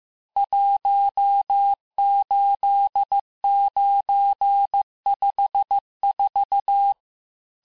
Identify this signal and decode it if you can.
Morse code, heard as a keyed tone, about 15 wpm, 800 Hz. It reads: 18954
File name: cours-du-lundi files / cours-1585-texte-chiffre-vitesse-14mpm